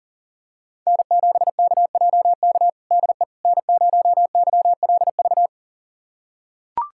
Morse code, heard as a keyed tone, about 40 wpm, 700 Hz. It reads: N7KJK DE N0YLV
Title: USA Callsign DE Callsign - 5 Characters - 40wpm